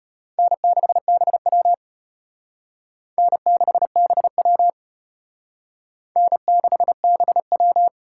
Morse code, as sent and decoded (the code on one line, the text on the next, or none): -. -.... -... .-- / -. -.... -... .-- / -. -.... -... .--
N6BW N6BW N6BW